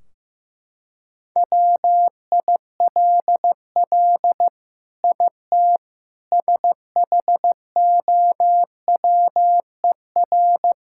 Morse code, heard as a keyed tone, about 16 wpm, 700 Hz. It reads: WILL IT SHOWER
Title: Sentences from Top 1600 Words 15wpm